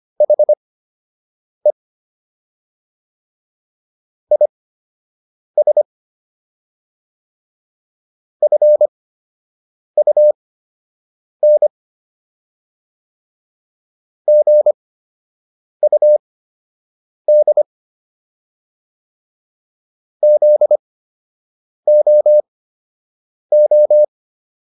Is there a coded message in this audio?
Yes: Morse code from 0.2 to 24.0 s, heard as a keyed tone, about 25 wpm, spaced as for 7 wpm, 600 Hz. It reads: HE IS FUN GUD ZOO